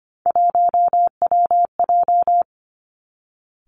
Morse code, heard as a keyed tone, about 25 wpm, 700 Hz. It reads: 1WJ
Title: Three Challenging Letter-Number Combinations 25wpm